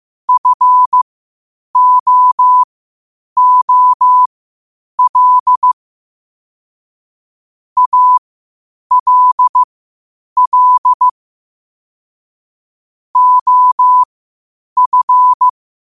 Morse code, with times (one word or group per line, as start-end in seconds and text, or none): 0.3-5.7 FOOL
7.8-11.1 ALL
13.2-15.5 OF